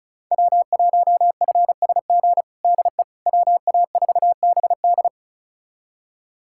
Morse code, heard as a keyed tone, about 36 wpm, 700 Hz. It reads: W1FSG DE WA4BD
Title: USA Callsign DE Callsign - 5 Characters - 35wpm